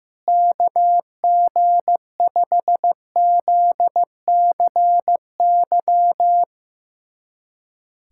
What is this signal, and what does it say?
Morse code, heard as a keyed tone, about 15 wpm, 700 Hz. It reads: KG5ZCY
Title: USA Callsign DE Callsign - 6 Characters - 15wpm